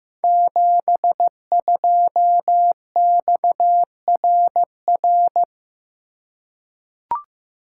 Morse code, heard as a keyed tone, about 15 wpm, 700 Hz. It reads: 72XRR
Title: Code Groups - 5 Letter:Numbers - 15wpm